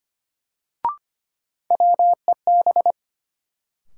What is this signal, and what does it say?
Morse code, heard as a keyed tone, about 25 wpm, 700 Hz. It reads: WEB
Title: Sentences from Top 1000 Words - Review Part 4 - New Word Drill - 25wpm